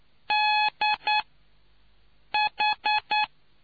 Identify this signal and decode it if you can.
Morse code, heard as a keyed tone, about 9 wpm, spaced as for 5 wpm, 850 Hz. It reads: DH